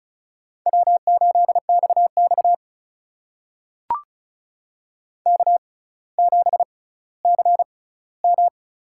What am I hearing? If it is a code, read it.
Morse code, heard as a keyed tone, about 36 wpm, spaced as for 18 wpm, 700 Hz. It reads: W8XX K7CM
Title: USA Call Signs - 4 Characters - 6x Character Spacing - 35wpm